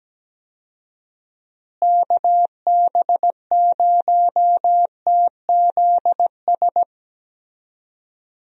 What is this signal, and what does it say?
Morse code, heard as a keyed tone, about 17 wpm, 700 Hz. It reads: KB0TZS
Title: USA Call Signs - 6 Characters - Repeated 3x - 17wpm